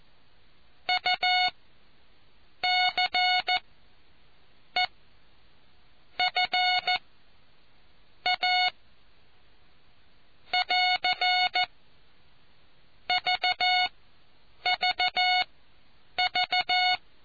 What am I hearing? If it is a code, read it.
Morse code, heard as a keyed tone, about 14 wpm, spaced as for 6 wpm, 730 Hz. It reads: UCEFA <AR> VVV